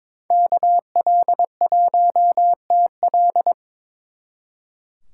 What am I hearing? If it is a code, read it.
Morse code, heard as a keyed tone, about 26 wpm, 700 Hz. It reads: KL1TL
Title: Speed Racing - USA Call Signs 22wpm